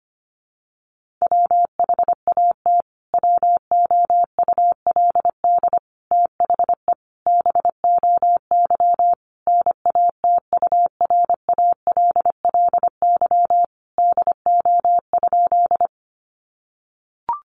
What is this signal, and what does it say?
Morse code, heard as a keyed tone, about 25 wpm, 700 Hz. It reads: WHAT WOULD THE BOY NATURALLY DO?
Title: Sentences from Top 1300 Words 25wpm